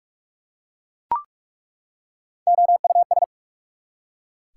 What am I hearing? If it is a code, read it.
Morse code, heard as a keyed tone, about 43 wpm, 690 Hz. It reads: OUS
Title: Prefixes and Suffixes 45wpm